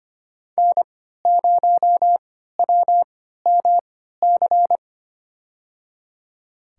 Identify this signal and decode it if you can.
Morse code, heard as a keyed tone, about 25 wpm, spaced as for 14 wpm, 700 Hz. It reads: N0WMC
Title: USA Call Signs - 5 Characters - 3x Character Spacing - 25wpm